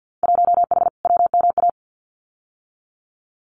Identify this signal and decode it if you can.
Morse code, heard as a keyed tone, about 50 wpm, 700 Hz. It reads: 1H WMA
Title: Winter Field Day Exchanges 50wpm